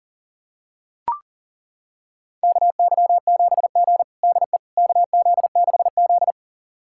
Morse code, heard as a keyed tone, about 37 wpm, 700 Hz. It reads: KY7G DE KZ6Z
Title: USA Callsign DE Callsign - 4 Characters - 40wpm